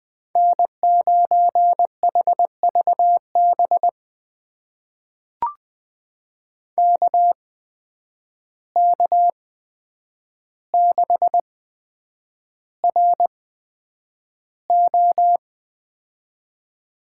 Morse code, heard as a keyed tone, about 20 wpm, spaced as for 9 wpm, 700 Hz. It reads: N9HVB KK6RO
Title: USA Call Signs - 5 Characters - 8x Character Spacing - 20wpm